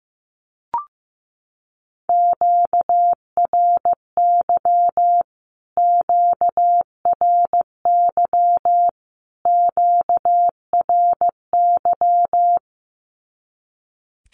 Morse code, heard as a keyed tone, about 15 wpm, 700 Hz. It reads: QRY QRY QRY
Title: Q-Codes Repeated 3x 15wpm